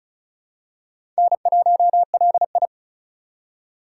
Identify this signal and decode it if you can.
Morse code, heard as a keyed tone, about 36 wpm, 700 Hz. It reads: N1LI
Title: USA Call Signs - 4x Character Spacing - 35wpm